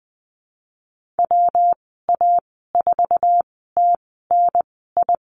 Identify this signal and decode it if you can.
Morse code, heard as a keyed tone, about 20 wpm, spaced as for 14 wpm, 700 Hz. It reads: WA4TNI
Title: 161 Lesson 26g - Letter g - Call Signs 20wpm